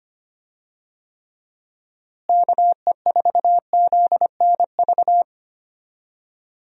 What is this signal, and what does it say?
Morse code, heard as a keyed tone, about 25 wpm, 700 Hz. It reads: KE4ZNV
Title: USA Call Signs - 6 Characters - Repeated 3x - 25wpm